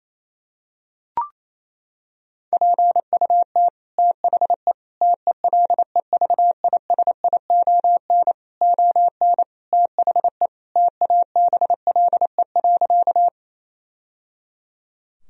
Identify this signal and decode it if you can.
Morse code, heard as a keyed tone, about 27 wpm, 700 Hz. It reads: PUT THE TELEVISION ON THE TABLE.
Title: Sentences from Top 1000 Words - Review Part 4 - 28wpm